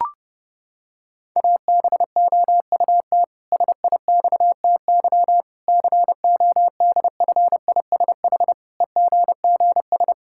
Morse code, heard as a keyed tone, about 30 wpm, 700 Hz. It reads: ABOUT SIXTY CODFISH EGGS
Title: Pangram Challenge v2 - 30wpm